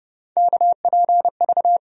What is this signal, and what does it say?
Morse code, heard as a keyed tone, about 30 wpm, 700 Hz. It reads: KPV
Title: USA Call Sign Postfixes - 3 Letters 30wpm